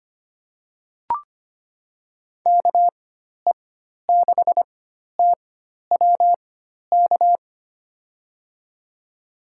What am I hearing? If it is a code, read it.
Morse code, heard as a keyed tone, about 25 wpm, spaced as for 12 wpm, 700 Hz. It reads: KE6TWK
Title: USA Call Signs - 6 Characters - 4x Character Spacing - 25wpm